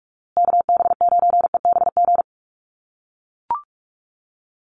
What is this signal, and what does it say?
Morse code, heard as a keyed tone, about 44 wpm, 700 Hz. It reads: KB9EBG
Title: USA Call Signs Repeated 2x 45wpm